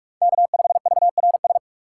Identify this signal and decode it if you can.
Morse code, heard as a keyed tone, about 46 wpm, 700 Hz. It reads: K5VRS
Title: USA Call Signs - 2x Character Spacing - 45wpm